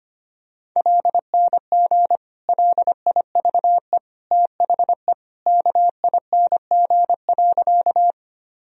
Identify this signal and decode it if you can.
Morse code, heard as a keyed tone, about 25 wpm, 700 Hz. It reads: LNG LIVE THE KING.
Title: Sentences from Top 400 Words and Abbreviations 25wpm